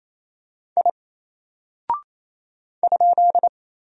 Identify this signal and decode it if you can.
Morse code, heard as a keyed tone, about 27 wpm, 700 Hz. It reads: I ?
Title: Single Character (Letter:Number:Punctuation:Prosign) 28wpm